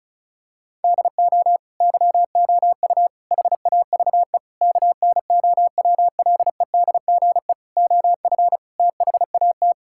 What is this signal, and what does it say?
Morse code, heard as a keyed tone, about 36 wpm, 700 Hz. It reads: DO YOU HAVE KNOWLEDGE OF THAT
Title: Sentences from Top 1000 Words - Review Part 6 - 35wpm